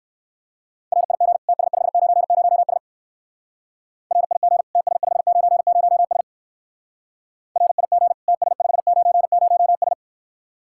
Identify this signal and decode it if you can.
Morse code, heard as a keyed tone, about 59 wpm, 700 Hz. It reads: RIG TS590S RIG TS590S RIG TS590S